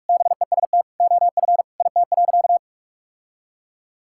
Morse code, heard as a keyed tone, about 46 wpm, 690 Hz. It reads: BEST OF IT.
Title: Sentences from Top 1000 Words - Review Part 5 - 45wpm